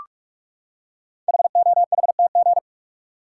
Morse code, heard as a keyed tone, about 44 wpm, 690 Hz. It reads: HOHTG
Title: Code Groups - 5 Letters - 45wpm